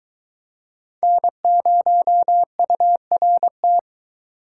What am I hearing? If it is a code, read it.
Morse code, heard as a keyed tone, about 22 wpm, 700 Hz. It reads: N0URT